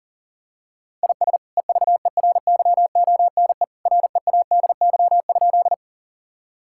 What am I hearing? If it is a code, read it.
Morse code, heard as a keyed tone, about 40 wpm, 700 Hz. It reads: IS EVERYONE READY?
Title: Sentences from Top 1000 Words - Review Part 5 - 40wpm